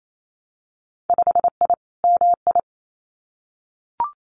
Morse code, heard as a keyed tone, about 28 wpm, 700 Hz. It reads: 5I MI
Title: Winter Field Day Exchanges Encore 28wpm